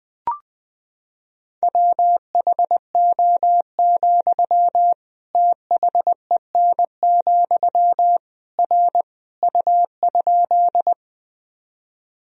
Morse code, heard as a keyed tone, about 20 wpm, 700 Hz. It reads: WHO, THEN, R U?
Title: Sentences from Top 100 Words and Abbreviations 20wpm